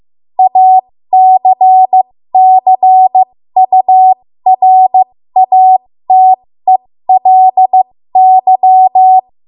Morse code, heard as a keyed tone, about 15 wpm, spaced as for 13 wpm, 750 Hz. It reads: ACCURATELY
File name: Morse / 13WPM